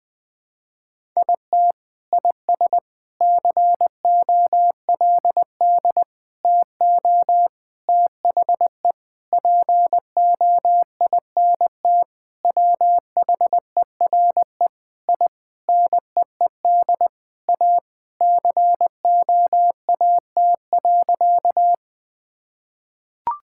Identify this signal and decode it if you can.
Morse code, heard as a keyed tone, about 20 wpm, 700 Hz. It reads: IT IS COLD TO THE POINT WHERE I NEED A COAT.
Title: Sentences with 4-Word Phrases - 20wpm